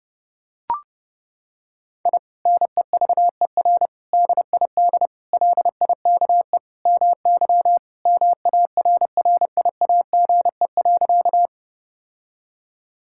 Morse code, heard as a keyed tone, about 30 wpm, 700 Hz. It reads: I NEVER DID LIKE MY MARRIAGE.
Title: Sentences from Top 1000 Words - Review Part 5 - 30wpm